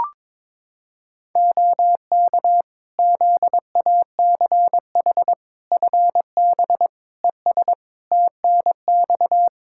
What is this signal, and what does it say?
Morse code, heard as a keyed tone, about 21 wpm, 700 Hz. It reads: OK ZACH FB ES TNX
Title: Practice QSO - Exchange 2 - Info - 22wpm